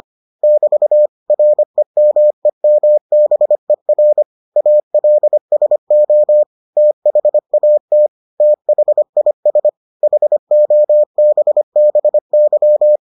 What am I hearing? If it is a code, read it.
Morse code, heard as a keyed tone, about 25 wpm, 600 Hz. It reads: <BT> REMEMBER ALSO THAT THIS HOBBY